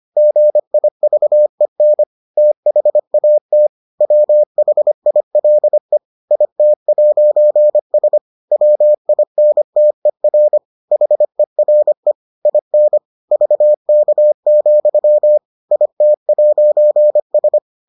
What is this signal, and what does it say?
Morse code, heard as a keyed tone, about 25 wpm, 600 Hz. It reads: GIVEN THAT WHILE IT'S WINTER HERE IN VK, IT'S